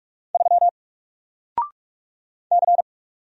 Morse code, heard as a keyed tone, about 45 wpm, 700 Hz. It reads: 3 C
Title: Single Letter-Number 45wpm